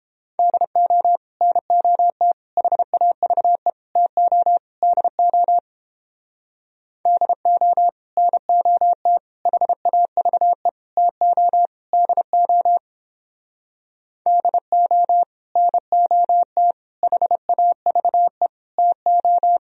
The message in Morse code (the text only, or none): DO NOT HAVE TO DO DO NOT HAVE TO DO DO NOT HAVE TO